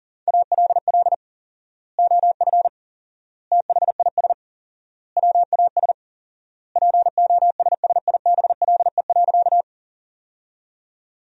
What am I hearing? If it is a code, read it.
Morse code, heard as a keyed tone, about 37 wpm, 700 Hz. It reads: ALL OF THIS WAS POSSIBLE.
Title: 4x Word-Spacing - Sentences from Top 500 Words - 40wpm